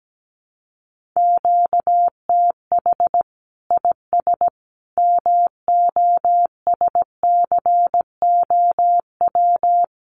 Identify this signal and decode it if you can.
Morse code, heard as a keyed tone, about 17 wpm, 700 Hz. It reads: QTH IS MOSCOW
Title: Common QSO Phrases 17wpm